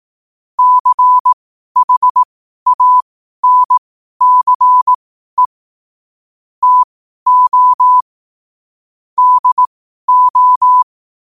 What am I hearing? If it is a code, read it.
Morse code, heard as a keyed tone, about 18 wpm, spaced as for 12 wpm, 1000 Hz. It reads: CHANCE TO DO
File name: qotdfiles / 12wpm